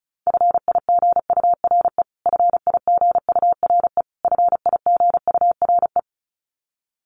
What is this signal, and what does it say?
Morse code, heard as a keyed tone, about 34 wpm, 700 Hz. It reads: FIGURE FIGURE FIGURE
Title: Top 500 Words Repeated 35wpm